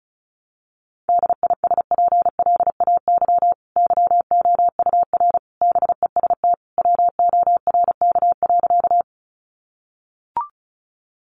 Morse code, heard as a keyed tone, about 34 wpm, 700 Hz. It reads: DISPLAY YOUR BEST WORK.
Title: Sentences from Top 1200 Words 35wpm